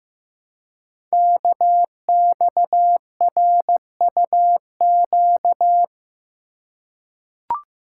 Morse code, heard as a keyed tone, about 15 wpm, 700 Hz. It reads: KXRUQ
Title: Code Groups - 5 Letters - 15wpm